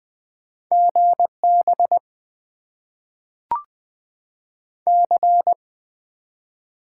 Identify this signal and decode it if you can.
Morse code, heard as a keyed tone, about 20 wpm, 700 Hz. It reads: GB C
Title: Top 90 QSO Elements - 8x Character Spacing - 20wpm